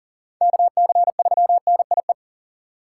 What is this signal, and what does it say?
Morse code, heard as a keyed tone, about 40 wpm, 700 Hz. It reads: KC3NIE
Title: K1USN SST QSO Practice - 40wpm